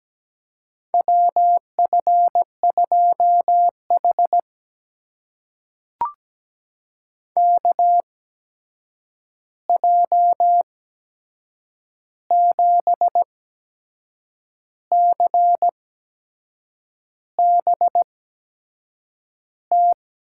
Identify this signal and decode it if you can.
Morse code, heard as a keyed tone, about 17 wpm, spaced as for 6 wpm, 700 Hz. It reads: WF2H KJ7CBT
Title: USA Call Signs - 8x Character Spacing - 17wpm